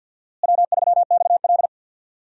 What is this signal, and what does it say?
Morse code, heard as a keyed tone, about 52 wpm, 700 Hz. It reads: W3XL
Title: USA Call Signs - 4 Characters - Repeated 3x - 50wpm